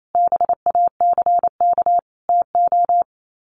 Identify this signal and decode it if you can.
Morse code, heard as a keyed tone, about 28 wpm, 690 Hz. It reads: BACK TO
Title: Sets of 5 Words Encore 28wpm